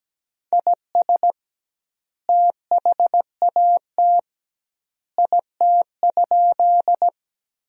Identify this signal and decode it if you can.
Morse code, heard as a keyed tone, about 17 wpm, 700 Hz. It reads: IS THAT IT?
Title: 2x Word-Spacing - Sentences from Top 100 Words - 17wpm